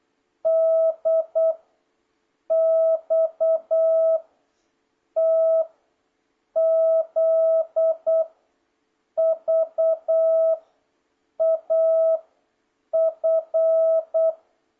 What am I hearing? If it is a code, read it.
Morse code, heard as a keyed tone, about 8 wpm, 650 Hz. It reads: D X T Z V A F